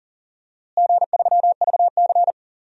Morse code, heard as a keyed tone, about 39 wpm, 700 Hz. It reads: G3VC
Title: DXCC-100 Call Signs - 4 Characters - Repeated 2x - 40wpm